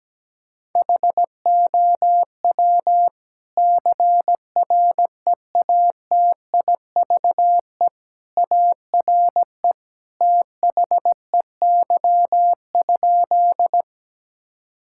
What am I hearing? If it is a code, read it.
Morse code, heard as a keyed tone, about 17 wpm, 700 Hz. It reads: HOW CREATIVE ARE THEY?
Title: Sentences from Top 1000 Words - Review Part 5 - 17wpm